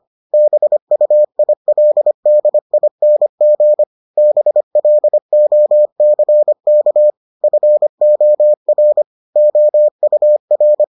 Morse code, heard as a keyed tone, about 25 wpm, 600 Hz. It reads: BUILDING BLOCK FOR OUR